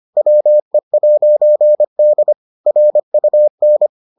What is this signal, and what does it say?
Morse code, heard as a keyed tone, about 25 wpm, 600 Hz. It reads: WE'D RUN